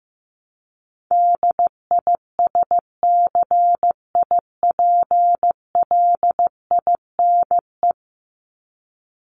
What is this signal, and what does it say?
Morse code, heard as a keyed tone, about 15 wpm, 700 Hz. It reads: DISCIPLINE
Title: Top 751-1000 Words 15wpm